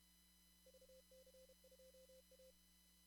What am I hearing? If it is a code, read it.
Morse code, heard as a keyed tone, about 31 wpm, 520 Hz. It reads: VC2A
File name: CW / kt5j wpx cw-op-k5pi